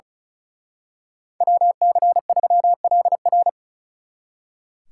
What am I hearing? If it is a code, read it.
Morse code, heard as a keyed tone, about 33 wpm, 700 Hz. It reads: WC3LR
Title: Speed Racing - USA 2x2 Call Signs 35wpm